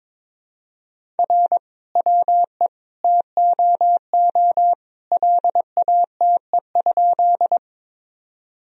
Morse code, heard as a keyed tone, about 22 wpm, 700 Hz. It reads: R WE TOO LATE?
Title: Sentences from Top 300 Words and Abbreviations 22wpm